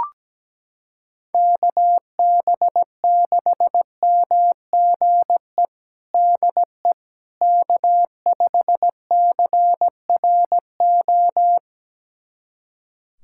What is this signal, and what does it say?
Morse code, heard as a keyed tone, about 17 wpm, 700 Hz. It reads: KB6MGE DE K5CRO
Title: USA Callsign DE Callsign 17wpm